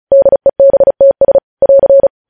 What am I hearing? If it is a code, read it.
Morse code, heard as a keyed tone, about 36 wpm, 550 Hz. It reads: DEBTS <AR>
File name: podcast / News headlines-1x-35x35